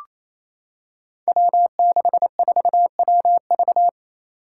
Morse code, heard as a keyed tone, about 28 wpm, 700 Hz. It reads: W64WV
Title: Five Challenging Letter-Number Combinations 28wpm